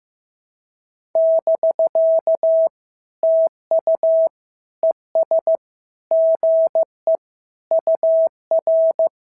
Morse code, heard as a keyed tone, about 15 wpm, 650 Hz. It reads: <BK> TU ES GE UR
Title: SOTA QSO Practice 15wpm